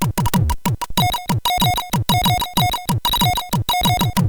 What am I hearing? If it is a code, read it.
There is no code.